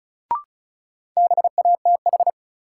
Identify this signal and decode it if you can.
Morse code, heard as a keyed tone, about 36 wpm, 700 Hz. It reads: BATH
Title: Sentences from Top 1600 Words - New Word Drill - 35wpm